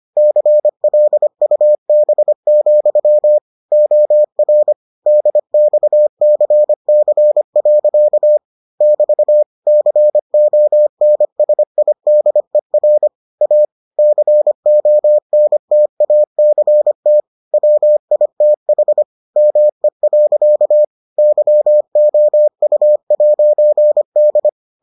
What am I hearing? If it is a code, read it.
Morse code, heard as a keyed tone, about 25 wpm, 600 Hz. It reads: CLUB, OR DXCC. <BT> CONSIDER A CONTACT WITH ME. YOU'D